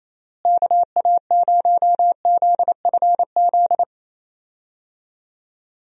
Morse code, heard as a keyed tone, about 28 wpm, 700 Hz. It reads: KA0ZFZ
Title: USA Call Signs - 6 Characters - Repeated 2x - 28wpm